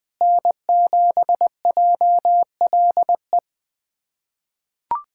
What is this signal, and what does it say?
Morse code, heard as a keyed tone, about 20 wpm, 700 Hz. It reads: N7JLE
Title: USA Callsign DE Callsign - 5 Characters - 20wpm